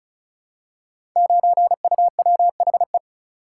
Morse code, heard as a keyed tone, about 36 wpm, 690 Hz. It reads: 9UWHE
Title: Code Groups - 5 Letter:Numbers - 35wpm